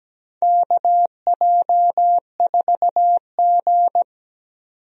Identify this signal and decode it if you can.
Morse code, heard as a keyed tone, about 17 wpm, 700 Hz. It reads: KJ4G
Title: USA Call Signs - 4 Characters - 17wpm